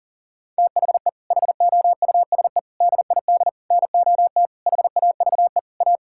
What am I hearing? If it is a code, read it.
Morse code, heard as a keyed tone, about 40 wpm, 700 Hz. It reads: THE HOUSE DID NOT HAVE A
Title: Sentences from Top 1600 Words 40wpm